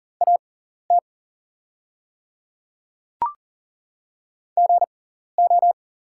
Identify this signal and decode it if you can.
Morse code, heard as a keyed tone, about 40 wpm, spaced as for 14 wpm, 690 Hz. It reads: AT GO
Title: 6x Character Spacing - Top 300 Words - 40wpm